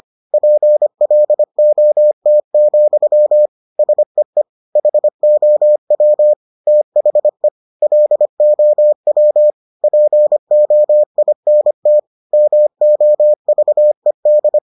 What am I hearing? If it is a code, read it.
Morse code, heard as a keyed tone, about 25 wpm, 600 Hz. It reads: PLOT, SEE HOW THE LOW POINT MOVED